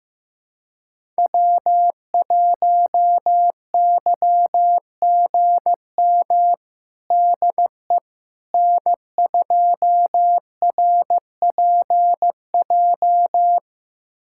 Morse code, heard as a keyed tone, about 15 wpm, 700 Hz. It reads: W1YGM DE N2RPJ
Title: USA Callsign DE Callsign - 5 Characters - 15wpm